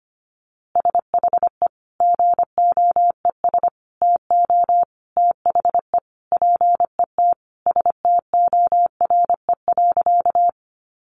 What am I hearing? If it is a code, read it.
Morse code, heard as a keyed tone, about 25 wpm, 700 Hz. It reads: SHE GOES TO THE PET STORE.
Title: Easy Phrases 25wpm